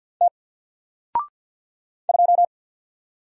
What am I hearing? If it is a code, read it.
Morse code, heard as a keyed tone, about 50 wpm, 700 Hz. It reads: T 2
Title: Single Letter-Number 50wpm